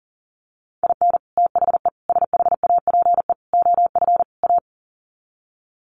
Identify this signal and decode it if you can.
Morse code, heard as a keyed tone, about 40 wpm, 700 Hz. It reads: IN THE SHAPE OF A
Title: Sets of 5 Words Encore 40wpm